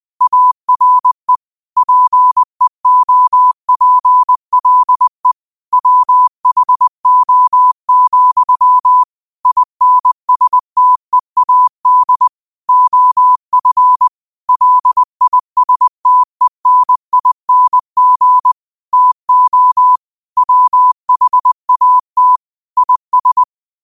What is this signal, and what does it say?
Morse code, heard as a keyed tone, about 20 wpm, 1000 Hz. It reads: ARE PEOPLE WHO, INSTEAD OF LISTENING TO WHAT IS